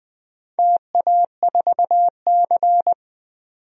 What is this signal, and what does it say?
Morse code, heard as a keyed tone, about 20 wpm, 700 Hz. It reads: TA4C
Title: DXCC-100 Call Signs - 4 Characters - 20wpm